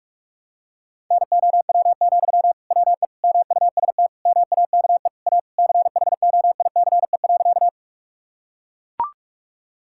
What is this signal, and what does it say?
Morse code, heard as a keyed tone, about 42 wpm, 700 Hz. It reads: NOW, WE MUST MAKE A CHOICE.
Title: Sentences from Top 1000 Words - Review Part 6 - 45wpm